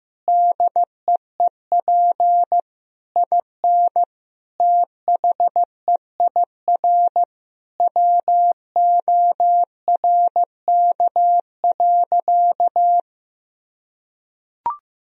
Morse code, heard as a keyed tone, about 15 wpm, 700 Hz. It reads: DEEP IN THEIR WORK.
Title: Sentences from Top 1000 Words - Review Part 3 - 15wpm